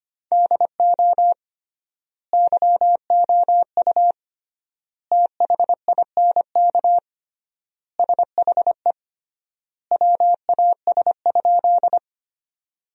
Morse code, heard as a keyed tone, about 25 wpm, 700 Hz. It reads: DO YOU THINK SHE WAS?
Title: 3x Word-Spacing - Sentences from Top 200 Words - 25wpm